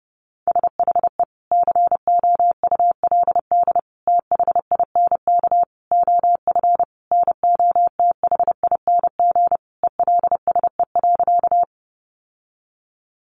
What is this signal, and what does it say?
Morse code, heard as a keyed tone, about 30 wpm, 700 Hz. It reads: SHE COULD THINK OF NOTHING ELSE.
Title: Sentences from Top 800 Words 30wpm